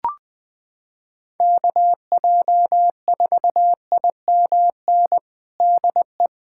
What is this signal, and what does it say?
Morse code, heard as a keyed tone, about 21 wpm, 700 Hz. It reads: KJ4IMN DE
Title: USA Callsign DE Callsign - 6 Characters - 20wpm